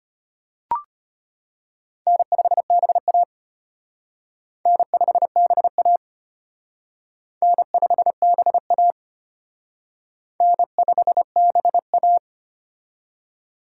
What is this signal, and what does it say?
Morse code, heard as a keyed tone, about 32 wpm, 700 Hz. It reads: N5BA N5BA N5BA N5BA